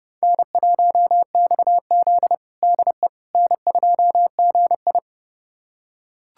Morse code, heard as a keyed tone, about 30 wpm, 700 Hz. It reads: N1XZ DE N2GI
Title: USA Callsign DE Callsign - 4 Characters - 30wpm